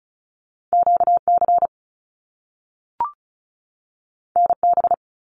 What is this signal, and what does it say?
Morse code, heard as a keyed tone, about 36 wpm, 700 Hz. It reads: QC NB
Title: Canadian Provinces and Territories 35wpm